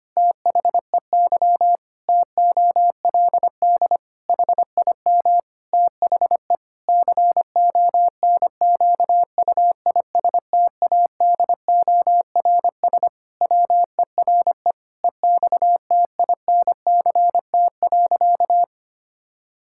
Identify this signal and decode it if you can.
Morse code, heard as a keyed tone, about 25 wpm, 700 Hz. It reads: THEY TOLD HIM THE CONQUISTADORS WERE EXTINCT.
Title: Pangram Challenge v2 - 25wpm